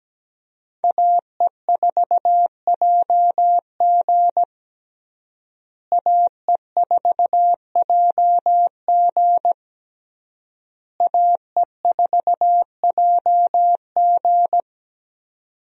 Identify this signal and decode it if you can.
Morse code, heard as a keyed tone, about 17 wpm, 700 Hz. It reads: AE4JG AE4JG AE4JG
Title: USA Call Signs - 5 Characters - Repeated 3x - 17wpm